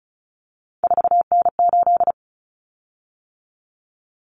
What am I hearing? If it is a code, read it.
Morse code, heard as a keyed tone, about 34 wpm, 700 Hz. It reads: 4N8
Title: Common Cut Numbers - 3-Digit 35wpm